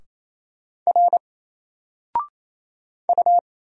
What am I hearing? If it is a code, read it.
Morse code, heard as a keyed tone, about 27 wpm, 700 Hz. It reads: R U
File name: Single-Letter-Number-28wpm